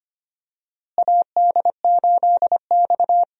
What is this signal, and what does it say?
Morse code, heard as a keyed tone, about 25 wpm, 700 Hz. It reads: AD8X
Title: USA Call Signs - 4 Characters - Repeated 3x - 25wpm